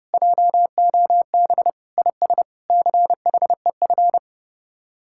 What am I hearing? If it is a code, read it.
Morse code, heard as a keyed tone, about 30 wpm, 700 Hz. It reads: JOB IS CHEF